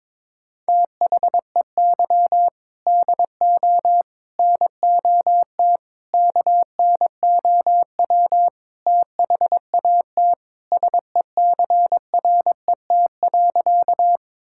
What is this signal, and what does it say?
Morse code, heard as a keyed tone, about 22 wpm, 700 Hz. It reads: THEY DO NOT KNOW THAT SECRET.
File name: practice / 22wpm / Sentences with 4-Word Phrases - 22wpm